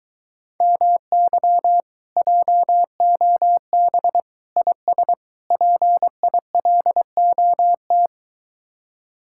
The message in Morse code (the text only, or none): MY JOB IS PILOT